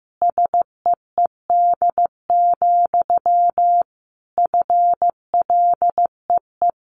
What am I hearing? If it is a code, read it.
Morse code, heard as a keyed tone, about 15 wpm, 700 Hz. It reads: SEED, FLEE
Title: A Sound Of Thunder 15wpm